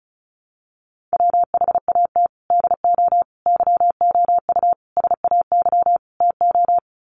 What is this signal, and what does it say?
Morse code, heard as a keyed tone, about 34 wpm, 700 Hz. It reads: WHAT DO YOU SAY TO
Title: Sets of 5 Words 35wpm